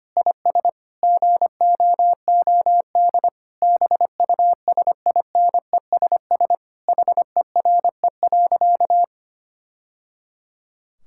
Morse code, heard as a keyed tone, about 25 wpm, 700 Hz. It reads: IS GOOD BUSINESS HERE.
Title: Sentences from Top 1000 Words - Review Part 5 - 25wpm